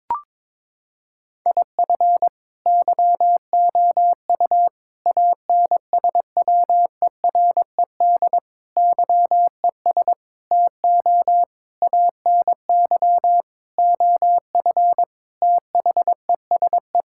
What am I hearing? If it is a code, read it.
Morse code, heard as a keyed tone, about 22 wpm, 700 Hz. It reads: IF YOU ANSWERED YES TO ANY OF THESE
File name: Sentences with 9-Word Phrases - 22wpm